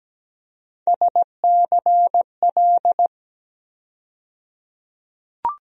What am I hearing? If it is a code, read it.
Morse code, heard as a keyed tone, about 17 wpm, 700 Hz. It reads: SCL
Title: USA Call Sign Postfixes - 3 Letters 17wpm